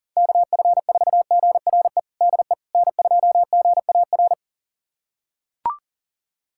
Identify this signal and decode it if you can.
Morse code, heard as a keyed tone, about 40 wpm, 700 Hz. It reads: KF4GRE DE N2GAR
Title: USA Callsign DE Callsign 40wpm